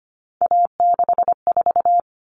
Morse code, heard as a keyed tone, about 25 wpm, 700 Hz. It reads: A64
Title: Common Cut Numbers - 3-Digit 25wpm